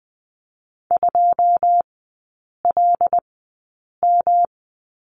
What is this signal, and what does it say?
Morse code, heard as a keyed tone, about 20 wpm, spaced as for 8 wpm, 700 Hz. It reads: 2LM
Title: 107 Lesson 18e - Letter m - Three Characters 20wpm